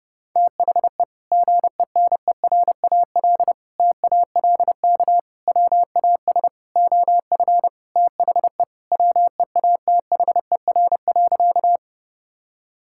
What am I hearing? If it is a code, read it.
Morse code, heard as a keyed tone, about 30 wpm, 700 Hz. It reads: THE GENERAL TALK WAS OF THE WEATHER.
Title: Sentences from Top 1000 Words - Review Part 5 - 30wpm